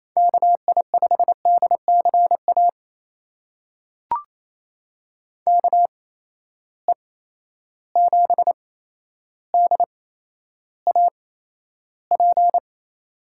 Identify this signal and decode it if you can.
Morse code, heard as a keyed tone, about 28 wpm, spaced as for 12 wpm, 700 Hz. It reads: KI5DCA KE7DAP
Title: USA Call Signs - 6 Characters - 8x Character Spacing - 28wpm